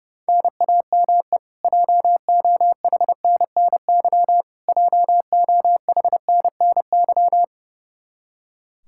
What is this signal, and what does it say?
Morse code, heard as a keyed tone, about 32 wpm, 700 Hz. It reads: NAME JOHNNY JOHNNY
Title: Practice QSO - Exchange 1 - Beginning - 30wpm